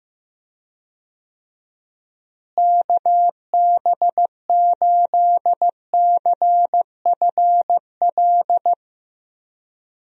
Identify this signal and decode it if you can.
Morse code, heard as a keyed tone, about 15 wpm, 700 Hz. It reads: KB8CFL
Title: USA Call Signs - 6 Characters - Repeated 2x - 15wpm